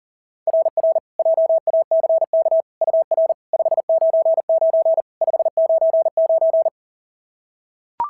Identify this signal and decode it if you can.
Morse code, heard as a keyed tone, about 40 wpm, 690 Hz. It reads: RR JACK UR 599 599
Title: SOTA QSO Practice 40wpm